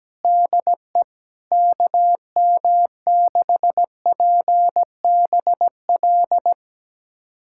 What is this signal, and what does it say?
Morse code, heard as a keyed tone, about 17 wpm, 700 Hz. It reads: DE KM6PBL
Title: USA Callsign DE Callsign - 6 Characters - 17wpm